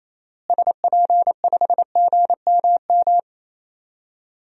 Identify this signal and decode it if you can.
Morse code, heard as a keyed tone, about 28 wpm, 700 Hz. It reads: SP5GMM
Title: DXCC-100 Call Signs - 6 Characters - Repeated 2x - 28wpm